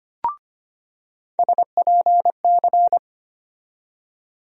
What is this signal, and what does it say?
Morse code, heard as a keyed tone, about 25 wpm, 700 Hz. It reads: SPC
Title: USA Call Sign Postfixes - 3 Letters 25wpm